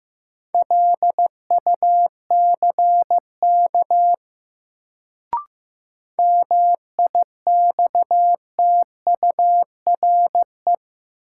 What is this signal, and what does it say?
Morse code, heard as a keyed tone, about 15 wpm, 700 Hz. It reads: LUCK MIXTURE
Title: Sentences from Top 1500 Words - New Word Drill Alt - 15wpm